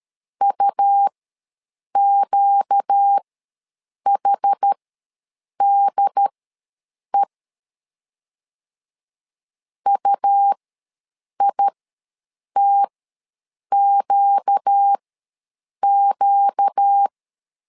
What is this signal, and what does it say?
Morse code, heard as a keyed tone, about 13 wpm, spaced as for 7 wpm, 800 Hz. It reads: UQHDE UITQQ
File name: LEZIONI / 4 Q ws 12wpm 1912